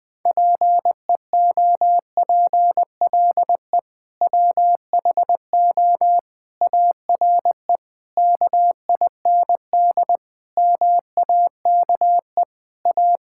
Morse code, heard as a keyed tone, about 21 wpm, 700 Hz. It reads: PEOPLE WHO ARE KIND MAKE A
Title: Sentences with 3-Word Phrases - 20wpm